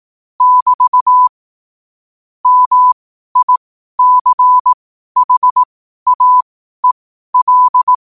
Morse code, heard as a keyed tone, about 18 wpm, spaced as for 12 wpm, 1000 Hz. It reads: <BT> MICHAEL
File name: qotdfiles / 12wpm